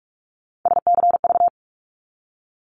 Morse code, heard as a keyed tone, about 46 wpm, 700 Hz. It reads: SCV
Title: ARRL Sections 45wpm